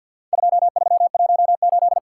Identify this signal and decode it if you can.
Morse code, heard as a keyed tone, about 50 wpm, 700 Hz. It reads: 2318
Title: Worked All Europe DX Contest - QTC Exchange - 50wpm